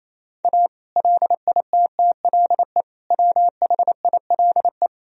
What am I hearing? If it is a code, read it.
Morse code, heard as a keyed tone, about 28 wpm, 700 Hz. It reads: A LITTLE WHILE
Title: Sentences from Top 1000 Words - Review Part 3 - 28wpm